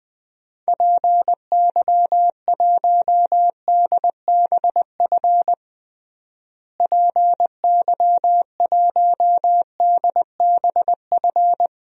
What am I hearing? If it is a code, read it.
Morse code, heard as a keyed tone, about 21 wpm, 700 Hz. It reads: PY1DBF PY1DBF
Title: DXCC-100 Call Signs - 6 Characters - Repeated 2x - 20wpm